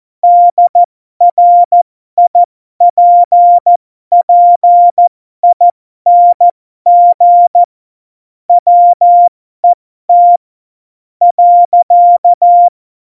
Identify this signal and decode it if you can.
Morse code, heard as a keyed tone, about 14 wpm, spaced as for 11 wpm, 700 Hz. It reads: DRIPPING WET.
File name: Morsekurs / Witze